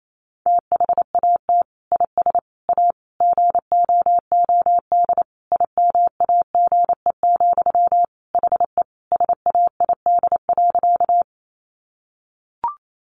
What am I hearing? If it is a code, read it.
Morse code, heard as a keyed tone, about 28 wpm, 700 Hz. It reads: THAT IS A GOOD IMAGE, HE SAID.
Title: Sentences from Top 700 Words 28wpm